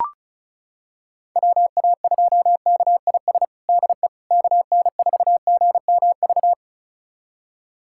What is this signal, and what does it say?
Morse code, heard as a keyed tone, about 34 wpm, 700 Hz. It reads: WA2KIS DE KN4GMV
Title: USA Callsign DE Callsign 35wpm